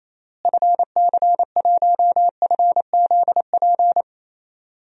Morse code, heard as a keyed tone, about 28 wpm, 700 Hz. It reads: FC1FZP
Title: DXCC-100 Call Signs - 6 Characters - Repeated 3x - 28wpm